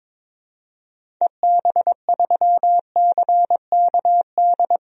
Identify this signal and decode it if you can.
Morse code, heard as a keyed tone, about 22 wpm, 700 Hz. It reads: EB3CKD
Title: DXCC-100 Call Signs - 6 Characters - 22wpm